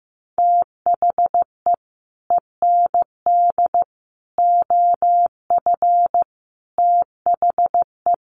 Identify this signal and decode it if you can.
Morse code, heard as a keyed tone, about 15 wpm, 700 Hz. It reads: THE END OF THE
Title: Sets of 5 Words Encore2 15wpm